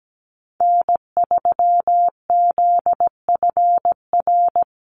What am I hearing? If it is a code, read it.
Morse code, heard as a keyed tone, about 17 wpm, 700 Hz. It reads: N3ZFR
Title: USA Call Signs 17wpm